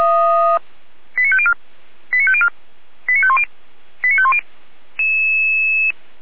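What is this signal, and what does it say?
Five-tone selective call (ZVEI), heard as quick runs of five tones, tones of about 70 ms. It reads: 89504 89504 89510 89510
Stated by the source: ZVEI five-tone call 89108 received on SDR, Feuerwehr Orlamünde.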